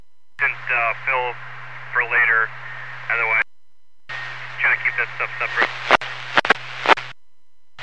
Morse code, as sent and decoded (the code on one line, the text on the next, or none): none